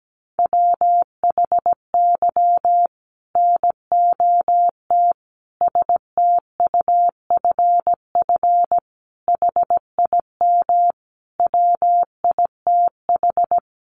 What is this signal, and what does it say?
Morse code, heard as a keyed tone, about 17 wpm, 700 Hz. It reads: WHY NOT STUFF HIM WITH
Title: Sentences from Top 1400 Words 17wpm